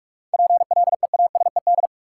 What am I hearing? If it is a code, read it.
Morse code, heard as a keyed tone, about 46 wpm, 700 Hz. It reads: PLEASED
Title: Seven Letter Words - 45wpm